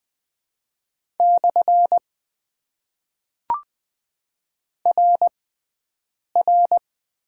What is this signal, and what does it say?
Morse code, heard as a keyed tone, about 20 wpm, spaced as for 7 wpm, 700 Hz. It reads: / RR
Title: Top 90 QSO Elements - 6x Character Spacing - 20wpm